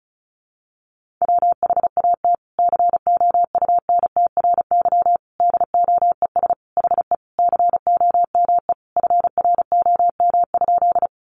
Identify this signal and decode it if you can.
Morse code, heard as a keyed tone, about 34 wpm, 700 Hz. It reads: WHAT COUNTRY DOES HE COME FROM?
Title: Sentences from Top 300 Words 35wpm